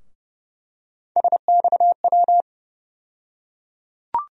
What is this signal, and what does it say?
Morse code, heard as a keyed tone, about 32 wpm, 700 Hz. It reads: SXW
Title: USA Call Sign Postfixes - 3 Letters 30wpm